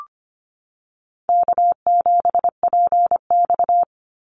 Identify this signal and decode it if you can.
Morse code, heard as a keyed tone, about 25 wpm, 700 Hz. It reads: K7PX
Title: NA SOTA Activator Callsigns 25wpm